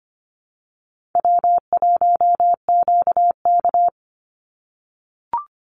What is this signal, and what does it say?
Morse code, heard as a keyed tone, about 25 wpm, 700 Hz. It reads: W1QK
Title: Easy Call Signs - 25wpm